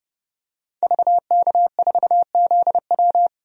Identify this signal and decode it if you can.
Morse code, heard as a keyed tone, about 30 wpm, 700 Hz. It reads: VK4ZW
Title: DXCC-100 Call Signs - 5 Characters - 30wpm